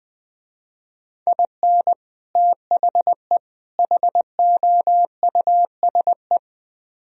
Morse code, heard as a keyed tone, about 21 wpm, 700 Hz. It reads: IN THE HOUSE
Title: Speed-Racing - Sets of 3 Words 17wpm